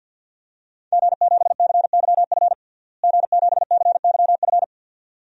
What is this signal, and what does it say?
Morse code, heard as a keyed tone, about 52 wpm, 700 Hz. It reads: G7CYF G7CYF
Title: DXCC-100 Call Signs - 5 Characters - Repeated 2x - 50wpm